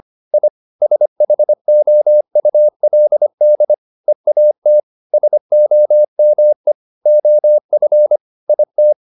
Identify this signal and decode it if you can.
Morse code, heard as a keyed tone, about 25 wpm, 600 Hz. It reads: I SHOULD EAT SOME OF IT